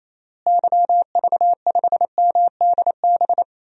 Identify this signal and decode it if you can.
Morse code, heard as a keyed tone, about 28 wpm, 700 Hz. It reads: YV5MDB
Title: DXCC-100 Call Signs - 6 Characters - Repeated 2x - 28wpm